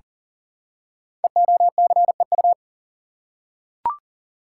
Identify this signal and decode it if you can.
Morse code, heard as a keyed tone, about 41 wpm, 700 Hz. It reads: EOCEU
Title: Code Groups - 5 Letter:Numbers - 40wpm